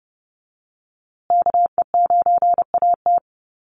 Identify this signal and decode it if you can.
Morse code, heard as a keyed tone, about 30 wpm, 700 Hz. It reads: KE9AT
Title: USA Call Signs Repeated 2x 30wpm